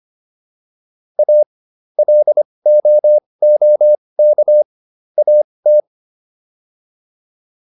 Morse code, heard as a keyed tone, about 25 wpm, spaced as for 20 wpm, 600 Hz. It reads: A LOOK AT